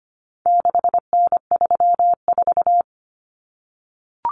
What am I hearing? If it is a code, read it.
Morse code, heard as a keyed tone, about 25 wpm, 700 Hz. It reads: 6N34
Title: Common Cut Numbers - 4-Digit 25wpm